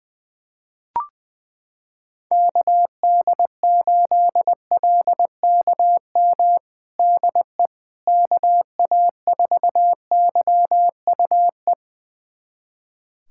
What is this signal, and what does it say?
Morse code, heard as a keyed tone, about 20 wpm, 700 Hz. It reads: KD8LKM DE KA4YUE
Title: USA Callsign DE Callsign - 6 Characters - 20wpm